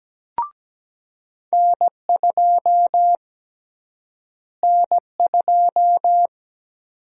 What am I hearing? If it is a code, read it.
Morse code, heard as a keyed tone, about 17 wpm, 700 Hz. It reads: N2 N2
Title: USA Call Sign Prefixes - Repeated 2x - 17wpm